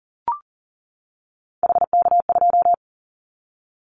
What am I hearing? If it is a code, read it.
Morse code, heard as a keyed tone, about 39 wpm, 700 Hz. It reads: HK2
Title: Callsign Prefixes for Top 100 DXCC - 40wpm